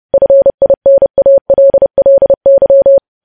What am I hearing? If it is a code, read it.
Morse code, heard as a keyed tone, about 30 wpm, 550 Hz. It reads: FINALLY
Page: news headlines in morse code at 30 wpm